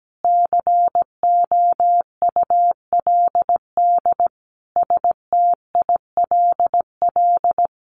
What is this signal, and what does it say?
Morse code, heard as a keyed tone, about 17 wpm, 700 Hz. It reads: COULD STILL
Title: Sentences from Top 1100 Words 17wpm